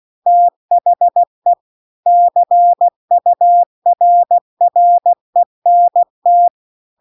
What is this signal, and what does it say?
Morse code, heard as a keyed tone, about 16 wpm, 700 Hz. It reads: THE CURRENT